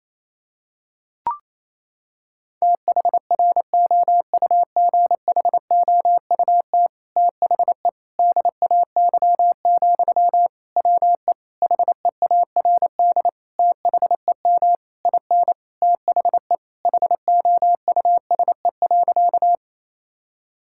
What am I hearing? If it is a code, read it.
Morse code, heard as a keyed tone, about 27 wpm, 700 Hz. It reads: THROUGHOUT THE DAY, WE HEARD THEM IN THE HOUSE.
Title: Sentences from Top 1000 Words - Review Part 5 - 28wpm